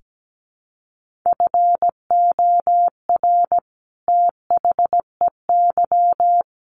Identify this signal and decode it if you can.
Morse code, heard as a keyed tone, about 17 wpm, 700 Hz. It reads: FOR THEY
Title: Sentences from Top 1400 Words 17wpm